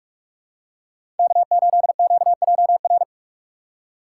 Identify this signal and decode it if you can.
Morse code, heard as a keyed tone, about 42 wpm, 700 Hz. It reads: K8QJR
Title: USA Call Signs - 5 Characters - 4x Character Spacing - 45wpm